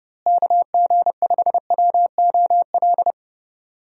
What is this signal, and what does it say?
Morse code, heard as a keyed tone, about 31 wpm, 700 Hz. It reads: KG5WOL
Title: USA Call Signs - 6 Characters - Repeated 3x - 30wpm